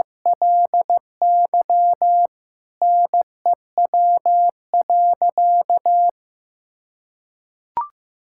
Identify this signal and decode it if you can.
Morse code, heard as a keyed tone, about 15 wpm, 700 Hz. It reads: LY NEW.
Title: Sentences from Top 1600 Words 15wpm